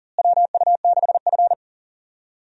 Morse code, heard as a keyed tone, about 38 wpm, 700 Hz. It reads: WU6F
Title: USA Call Signs - 4 Characters - Repeated 2x - 40wpm